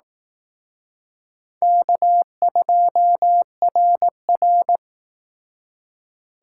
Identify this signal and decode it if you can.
Morse code, heard as a keyed tone, about 18 wpm, 700 Hz. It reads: K2RR
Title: Speed Racing - USA 1x2 Call Signs 15wpm